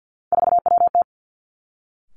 Morse code, heard as a keyed tone, about 50 wpm, 700 Hz. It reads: 4WT
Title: Three Challenging Letter-Number Combinations 50wpm